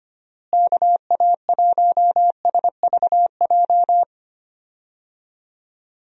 Morse code, heard as a keyed tone, about 25 wpm, 700 Hz. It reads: KA1SVJ
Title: USA Call Signs - 6 Characters - Repeated 2x - 25wpm